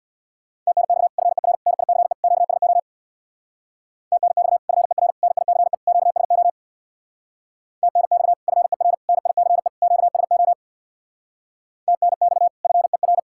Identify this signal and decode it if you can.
Morse code, heard as a keyed tone, about 60 wpm, 700 Hz. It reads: TNX FER NICE QSO TNX FER NICE QSO TNX FER NICE QSO TNX FER
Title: Speed-Racing Common QSO Phrases - 50wpm